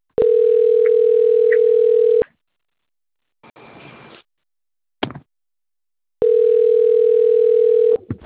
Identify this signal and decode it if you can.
Ringback tone (North America), 440 + 480 Hz.